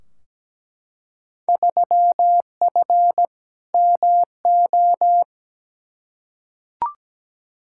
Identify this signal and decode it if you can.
Morse code, heard as a keyed tone, about 17 wpm, 700 Hz. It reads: 3F MO
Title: Field Day Exchanges Encore - 17wpm